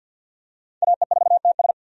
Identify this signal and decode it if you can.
Morse code, heard as a keyed tone, about 49 wpm, 700 Hz. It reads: AE4TS
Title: USA Call Signs - 5 Characters - 50wpm